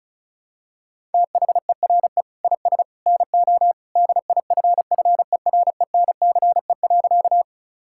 Morse code, heard as a keyed tone, about 36 wpm, 700 Hz. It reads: THERE IS NO DIFFERENCE.
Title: Sentences from Top 1000 Words - Review Part 5 - 35wpm